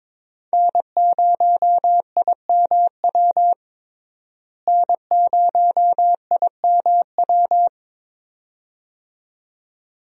Morse code, heard as a keyed tone, about 22 wpm, 700 Hz. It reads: N0IMW N0IMW
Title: USA Call Signs - 5 Characters - Repeated 2x - 22wpm